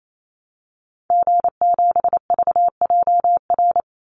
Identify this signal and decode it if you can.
Morse code, heard as a keyed tone, about 28 wpm, 700 Hz. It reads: G7VJR
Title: Call Signs Repeated 3x - 28wpm